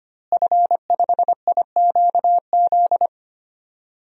Morse code, heard as a keyed tone, about 25 wpm, 700 Hz. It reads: F5IQZ
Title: DXCC-100 Call Signs - 5 Characters - Repeated 2x - 25wpm